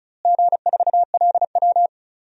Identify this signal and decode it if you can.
Morse code, heard as a keyed tone, about 36 wpm, 700 Hz. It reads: G4LW
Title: DXCC-100 Call Signs - 4 Characters - Repeated 2x - 35wpm